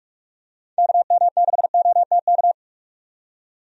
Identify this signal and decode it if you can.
Morse code, heard as a keyed tone, about 44 wpm, 700 Hz. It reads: KM6OTK
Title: USA Call Signs - 6 Characters - 45wpm